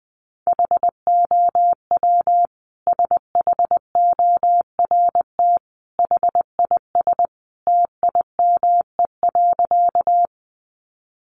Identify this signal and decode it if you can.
Morse code, heard as a keyed tone, about 20 wpm, 700 Hz. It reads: HOW SHORT HIS TIME.
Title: Sentences from Top 400 Words 20wpm